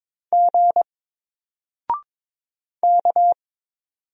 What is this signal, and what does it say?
Morse code, heard as a keyed tone, about 22 wpm, 700 Hz. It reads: G K
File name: Single-Letter-Number-22wpm